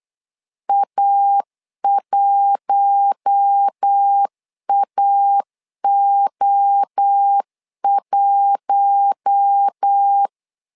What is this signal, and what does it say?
Morse code, heard as a keyed tone, about 8 wpm, 800 Hz. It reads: A1AO1
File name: LEZIONI / 4 Q1 8wpm 3035